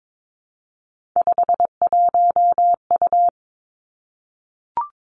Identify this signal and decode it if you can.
Morse code, heard as a keyed tone, about 22 wpm, 700 Hz. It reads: 51U
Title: Three Challenging Letter-Number Combinations 22wpm